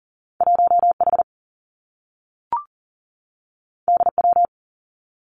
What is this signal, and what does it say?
Morse code, heard as a keyed tone, about 40 wpm, 700 Hz. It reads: 1H DW
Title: Two Challenging Letter-Number Combinations 40wpm